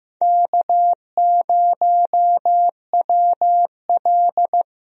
Morse code, heard as a keyed tone, about 16 wpm, 700 Hz. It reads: K0WL
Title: USA Call Signs - 4 Characters - Repeated 3x - 15wpm